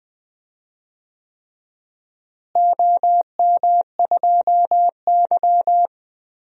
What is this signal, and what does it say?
Morse code, heard as a keyed tone, about 20 wpm, 700 Hz. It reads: OM2Y
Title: DXCC-100 Call Signs - 4 Characters - Repeated 3x - 20wpm